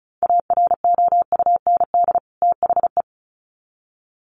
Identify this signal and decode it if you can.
Morse code, heard as a keyed tone, about 34 wpm, 700 Hz. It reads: AROUND THE
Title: Sets of 2 Words 35wpm